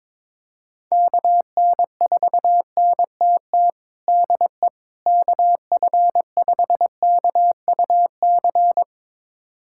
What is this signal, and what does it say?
Morse code, heard as a keyed tone, about 21 wpm, 700 Hz. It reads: KN4NTT DE KF5KUC
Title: USA Callsign DE Callsign 22wpm